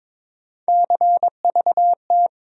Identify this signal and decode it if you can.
Morse code, heard as a keyed tone, about 22 wpm, 700 Hz. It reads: CVT
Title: USA Call Sign Postfixes - 3 Letters 22wpm